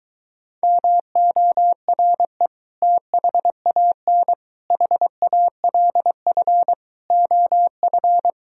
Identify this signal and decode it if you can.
Morse code, heard as a keyed tone, about 23 wpm, 700 Hz. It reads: MORE THAN HALF OF